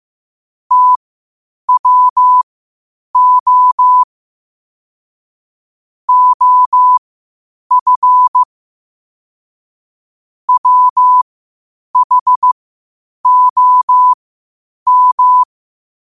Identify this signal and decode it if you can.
Morse code, heard as a keyed tone, about 15 wpm, spaced as for 8 wpm, 1000 Hz. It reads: TWO OF WHOM
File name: qotdfiles / 8wpm